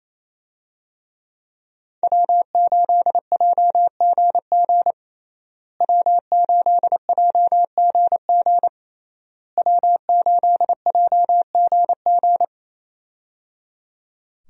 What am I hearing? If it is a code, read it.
Morse code, heard as a keyed tone, about 28 wpm, 700 Hz. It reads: W8JGG W8JGG W8JGG
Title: USA Call Signs - 5 Characters - Repeated 3x - 28wpm